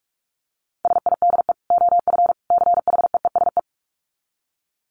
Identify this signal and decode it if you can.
Morse code, heard as a keyed tone, about 44 wpm, 700 Hz. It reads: SIDE OF CHEESE
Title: Dit Challenge 45wpm